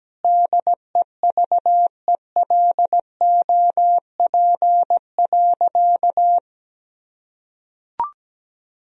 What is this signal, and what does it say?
Morse code, heard as a keyed tone, about 17 wpm, 700 Hz. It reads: DEVELOP.
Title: Sentences from Top 1000 Words - Review Part 2 - 17wpm